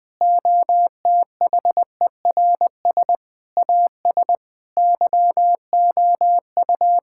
Morse code, heard as a keyed tone, about 20 wpm, 700 Hz. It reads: OTHERS AS YOU
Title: Sentences with 8-Word Phrases - 20wpm